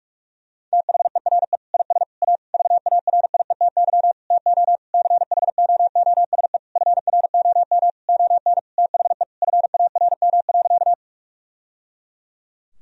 Morse code, heard as a keyed tone, about 41 wpm, 700 Hz. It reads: THERE IS A VARIETY TO CHOOSE FROM ON THE FARM.
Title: Sentences from Top 1000 Words - Review Part 4 - 45wpm